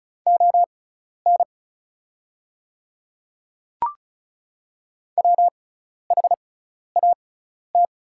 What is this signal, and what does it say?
Morse code, heard as a keyed tone, about 36 wpm, spaced as for 12 wpm, 690 Hz. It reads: ON WHAT
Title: 6x Character Spacing - Top 300 Words - 35wpm